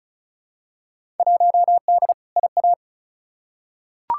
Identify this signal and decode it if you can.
Morse code, heard as a keyed tone, about 36 wpm, 700 Hz. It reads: 1D IA
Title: Field Day Exchanges - 35wpm